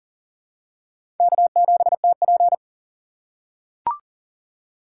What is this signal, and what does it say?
Morse code, heard as a keyed tone, about 40 wpm, 700 Hz. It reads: K7TP
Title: USA Call Signs - 4 Characters - 2x Character Spacing - 40wpm